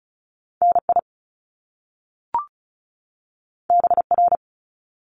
Morse code, heard as a keyed tone, about 36 wpm, 700 Hz. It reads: NI BR
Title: Top Two Letter Combinations in Words 35wpm